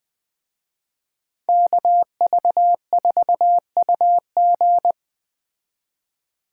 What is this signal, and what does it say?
Morse code, heard as a keyed tone, about 20 wpm, 700 Hz. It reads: KV4UG